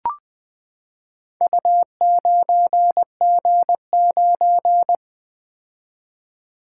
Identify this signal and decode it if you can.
Morse code, heard as a keyed tone, about 20 wpm, 700 Hz. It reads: U9G9
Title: Four Letter-Number Combinations - 20wpm